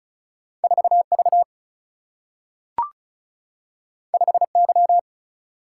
Morse code, heard as a keyed tone, about 36 wpm, 700 Hz. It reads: 4 V 5 Y
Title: Single-Letter Single-Number 35wpm